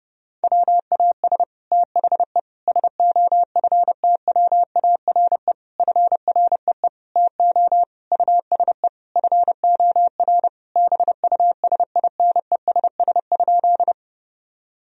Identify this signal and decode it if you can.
Morse code, heard as a keyed tone, about 30 wpm, 700 Hz. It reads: WAS THE SOFTWARE FREE TO USE FOR BUSINESS?
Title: Sentences from Top 1000 Words - Review Part 3 - 30wpm